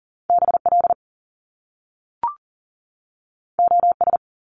Morse code, heard as a keyed tone, about 40 wpm, 700 Hz. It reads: BL OS
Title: Top Two Letter Combinations in Words 40wpm